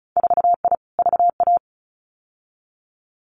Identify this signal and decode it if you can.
Morse code, heard as a keyed tone, about 36 wpm, 700 Hz. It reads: 4I VA
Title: Winter Field Day Exchanges 35wpm